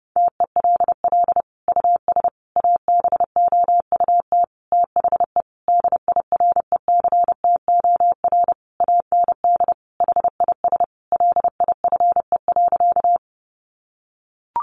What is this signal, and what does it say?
Morse code, heard as a keyed tone, about 30 wpm, 700 Hz. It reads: TELL US ABOUT THE DIRECTOR AND HIS LIFE.
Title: Sentences from Top 1300 Words 30wpm